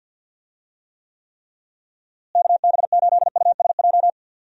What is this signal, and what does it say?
Morse code, heard as a keyed tone, about 52 wpm, 700 Hz. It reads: KB8USJ
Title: USA Call Signs - 6 Characters - Repeated 3x - 50wpm